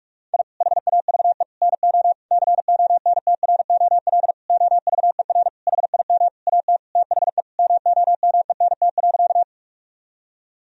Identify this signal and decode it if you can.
Morse code, heard as a keyed tone, about 42 wpm, 700 Hz. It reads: I HAVE NO CONTROL OVER HIM AT THE MOMENT.
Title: Sentences from Top 1000 Words - Review Part 3 - 45wpm